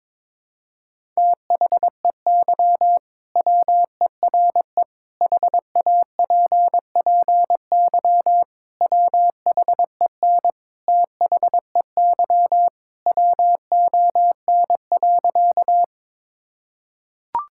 Morse code, heard as a keyed tone, about 22 wpm, 700 Hz. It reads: THEY WERE HAPPY WHEN THEY WON.
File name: practice / 22wpm / Sentences with 2-Word Phrases - 22wpm